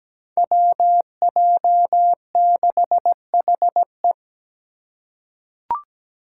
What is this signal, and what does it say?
Morse code, heard as a keyed tone, about 17 wpm, 700 Hz. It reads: WJ6HE
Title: Five Challenging Letter-Number Combinations 17wpm